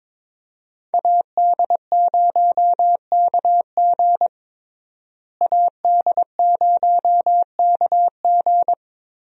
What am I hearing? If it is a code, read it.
Morse code, heard as a keyed tone, about 22 wpm, 700 Hz. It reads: AD0KG AD0KG
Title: USA Call Signs - 5 Characters - Repeated 3x - 22wpm